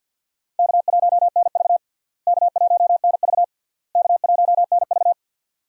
Morse code, heard as a keyed tone, about 51 wpm, 700 Hz. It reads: K1NV K1NV K1NV
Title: USA Call Signs - 4 Characters - Repeated 3x - 50wpm